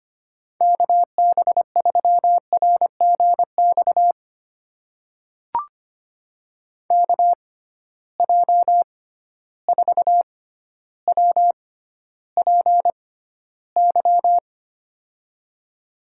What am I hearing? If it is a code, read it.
Morse code, heard as a keyed tone, about 25 wpm, spaced as for 13 wpm, 700 Hz. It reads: KB3RGX KJ4WPY
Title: USA Call Signs - 6 Characters - 6x Character Spacing - 25wpm